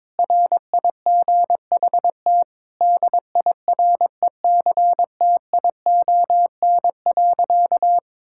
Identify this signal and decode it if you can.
Morse code, heard as a keyed tone, about 22 wpm, 700 Hz. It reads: RIGHT DIRECTION.
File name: Sentences with 7-Word Phrases - 22wpm